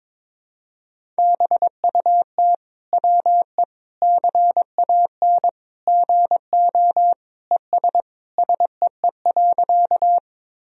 Morse code, heard as a keyed tone, about 22 wpm, 700 Hz. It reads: BUT WE CAN GO ES SEE.
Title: Sentences from Top 100 Words and Abbreviations 22wpm